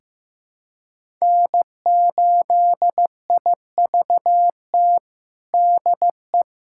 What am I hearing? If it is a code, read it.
Morse code, heard as a keyed tone, about 15 wpm, 700 Hz. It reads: N8IVT DE
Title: USA Callsign DE Callsign - 5 Characters - 15wpm